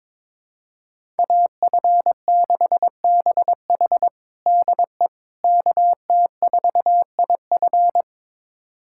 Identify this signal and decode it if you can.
Morse code, heard as a keyed tone, about 21 wpm, 700 Hz. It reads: AF6BH DE KT4IF
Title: USA Callsign DE Callsign - 5 Characters - 22wpm